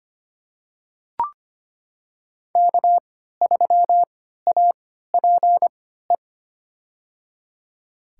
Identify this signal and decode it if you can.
Morse code, heard as a keyed tone, about 25 wpm, spaced as for 14 wpm, 700 Hz. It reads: K3APE
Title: USA Call Signs - 5 Characters - 3x Character Spacing - 25wpm